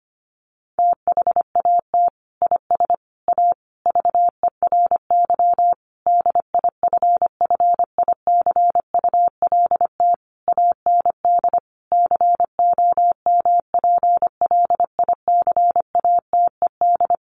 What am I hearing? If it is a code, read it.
Morse code, heard as a keyed tone, about 25 wpm, 700 Hz. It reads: THAT IS A VERY DIFFICULT AND COMPLICATED
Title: Sentences from Top 1200 Words 25wpm